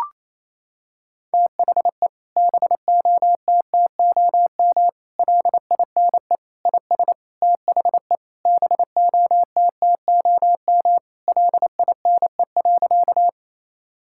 Morse code, heard as a keyed tone, about 28 wpm, 700 Hz. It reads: THE BOTTOM LINE IS THE BOTTOM LINE.
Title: Sentences from Top 1000 Words - Review Part 6 - 28wpm